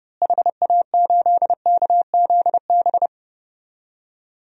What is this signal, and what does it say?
Morse code, heard as a keyed tone, about 30 wpm, 700 Hz. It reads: HA8KZB
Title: DXCC-100 Call Signs - 6 Characters - Repeated 3x - 30wpm